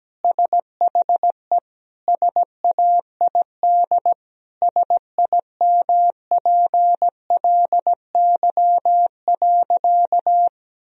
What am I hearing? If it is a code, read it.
Morse code, heard as a keyed tone, about 17 wpm, 700 Hz. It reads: SHE SAID SIMPLY.
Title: Sentences from Top 1000 Words - Review Part 4 - 17wpm